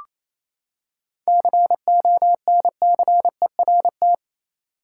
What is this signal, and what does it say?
Morse code, heard as a keyed tone, about 28 wpm, 700 Hz. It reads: CONCERT
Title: Seven Letter Words - 28wpm